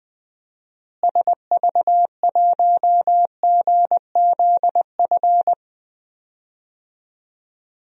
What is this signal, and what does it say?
Morse code, heard as a keyed tone, about 20 wpm, 700 Hz. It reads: SV1GZF
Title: DXCC-100 Call Signs - 6 Characters - Repeated 3x - 20wpm